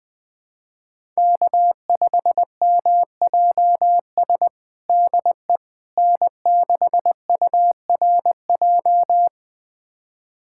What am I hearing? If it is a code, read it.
Morse code, heard as a keyed tone, about 21 wpm, 700 Hz. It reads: K5MJS DE N6URJ
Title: USA Callsign DE Callsign - 5 Characters - 20wpm